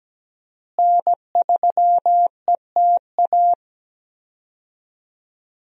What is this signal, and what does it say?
Morse code, heard as a keyed tone, about 17 wpm, 700 Hz. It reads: N3ETA
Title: USA Call Signs - 5 Characters - Repeated 2x - 17wpm